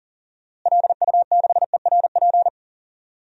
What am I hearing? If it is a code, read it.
Morse code, heard as a keyed tone, about 37 wpm, 700 Hz. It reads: LU6ERP
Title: DXCC-100 Call Signs - 6 Characters - Repeated 2x - 40wpm